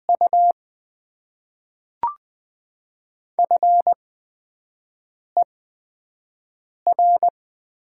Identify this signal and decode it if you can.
Morse code, heard as a keyed tone, about 20 wpm, spaced as for 5 wpm, 700 Hz. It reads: U FER
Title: Top 20 QSO Elements - 8x Character Spacing - 20wpm